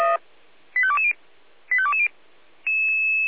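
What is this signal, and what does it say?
Five-tone selective call (ZVEI), heard as quick runs of five tones, tones of about 70 ms. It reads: 85220 85220
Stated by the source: ZVEI five-tone selective call received with an SDR.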